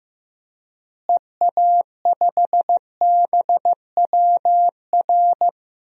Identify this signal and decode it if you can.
Morse code, heard as a keyed tone, about 14 wpm, 700 Hz. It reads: EA5BWR
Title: DXCC-100 Call Signs - 6 Characters - Repeated 3x - 15wpm